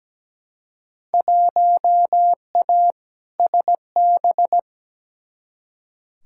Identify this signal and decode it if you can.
Morse code, heard as a keyed tone, about 17 wpm, 700 Hz. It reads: 1A SB
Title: Field Day Exchanges Encore - 17wpm